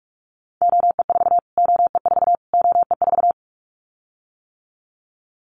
Morse code, heard as a keyed tone, about 44 wpm, 700 Hz. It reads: OE4 OE4 OE4
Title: Callsign Prefixes for Top 25 DXCC - Repeated 3x - 45wpm